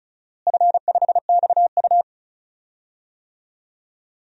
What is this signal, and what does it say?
Morse code, heard as a keyed tone, about 36 wpm, 700 Hz. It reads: F5XU
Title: DXCC-100 Call Signs - 4 Characters - Repeated 2x - 35wpm